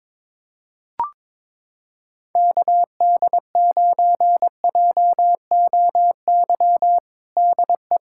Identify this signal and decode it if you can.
Morse code, heard as a keyed tone, about 22 wpm, 700 Hz. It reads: KD9JOY DE
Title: USA Callsign DE Callsign - 6 Characters - 22wpm